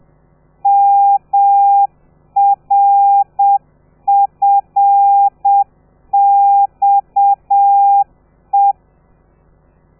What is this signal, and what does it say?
Morse code, heard as a keyed tone, about 7 wpm, 800 Hz. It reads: MRFXE
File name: telegrafia / i2tfj / Lez13 30